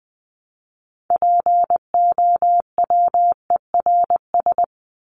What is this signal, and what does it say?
Morse code, heard as a keyed tone, about 20 wpm, 700 Hz. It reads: POWERS
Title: ES:S Suffix - Top 100 Words with Base Words 20wpm